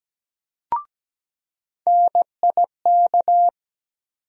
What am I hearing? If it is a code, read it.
Morse code, heard as a keyed tone, about 17 wpm, 700 Hz. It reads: NIK
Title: USA Call Sign Postfixes - 3 Letters 17wpm